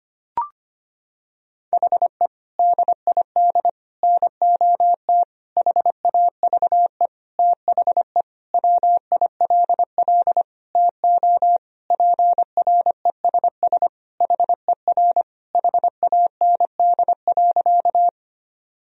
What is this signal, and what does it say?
Morse code, heard as a keyed tone, about 25 wpm, 700 Hz. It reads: HE DID NOT HAVE THE WILL TO PRESS HER HAND.
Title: Sentences from Top 1000 Words - Review Part 1 - 25wpm